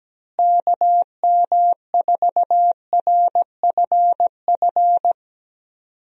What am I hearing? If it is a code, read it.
Morse code, heard as a keyed tone, about 17 wpm, 700 Hz. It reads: KM4RFF
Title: USA Call Signs - 6 Characters - Repeated 3x - 17wpm